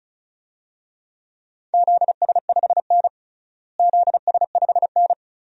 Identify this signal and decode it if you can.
Morse code, heard as a keyed tone, about 36 wpm, 700 Hz. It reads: ZS5N ZS5N
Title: DXCC-100 Call Signs - 4 Characters - Repeated 3x - 35wpm